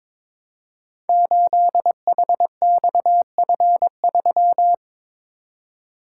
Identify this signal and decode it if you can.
Morse code, heard as a keyed tone, about 22 wpm, 700 Hz. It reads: 8HXF3
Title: Code Groups - 5 Letter:Numbers - 22wpm